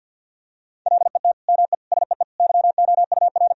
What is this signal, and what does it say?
Morse code, heard as a keyed tone, about 50 wpm, 700 Hz. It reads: LET ME SEE YOUR
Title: Sentences from Top 1000 Words - Review Part 2 - 50wpm